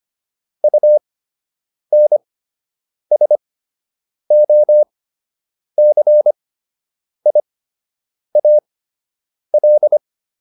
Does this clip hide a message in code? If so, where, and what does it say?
Yes: Morse code from 0.6 to 10.0 s, heard as a keyed tone, about 25 wpm, spaced as for 8 wpm, 600 Hz. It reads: UNSOCIAL